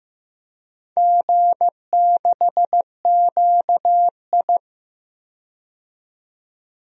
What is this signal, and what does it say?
Morse code, heard as a keyed tone, about 15 wpm, 700 Hz. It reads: G6QI
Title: DXCC-100 Call Signs - 4 Characters - 15wpm